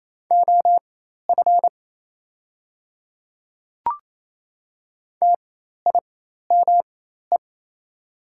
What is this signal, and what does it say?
Morse code, heard as a keyed tone, about 28 wpm, spaced as for 13 wpm, 700 Hz. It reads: OF TIME
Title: 4x Character Spacing - Top 200 Words - 28wpm